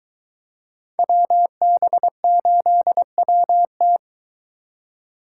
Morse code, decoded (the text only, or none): WB8WT